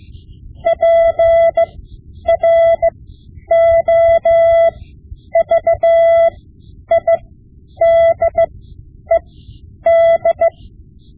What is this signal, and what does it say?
Morse code, heard as a keyed tone, about 16 wpm, spaced as for 9 wpm, 670 Hz. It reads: PROVIDED
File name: Morse Lessons / Morse Lessons / Tape-5B